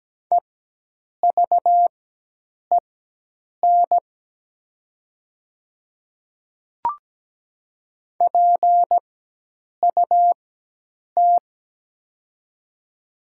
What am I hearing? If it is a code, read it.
Morse code, heard as a keyed tone, about 17 wpm, spaced as for 8 wpm, 700 Hz. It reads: EVEN PUT
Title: 4x Character Spacing - Top 200 Words - 17wpm